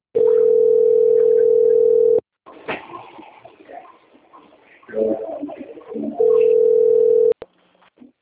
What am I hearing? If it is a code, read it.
Ringback tone (North America), 440 + 480 Hz.